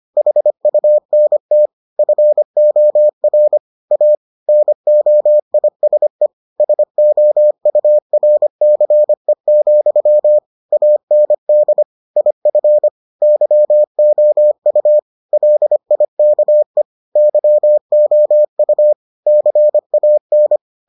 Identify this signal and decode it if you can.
Morse code, heard as a keyed tone, about 25 wpm, 600 Hz. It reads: HUNT FOR A NOISE SOURCE, AND IF YOU LIKE YOU CAN